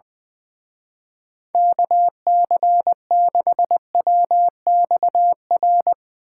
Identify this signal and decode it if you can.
Morse code, heard as a keyed tone, about 20 wpm, 700 Hz. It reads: KC6WXR
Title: Speed Racing - USA Call Signs 20wpm